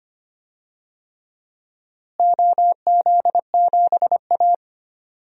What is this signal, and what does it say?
Morse code, heard as a keyed tone, about 25 wpm, 700 Hz. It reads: OZ7A
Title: DXCC-100 Call Signs - 4 Characters - Repeated 3x - 25wpm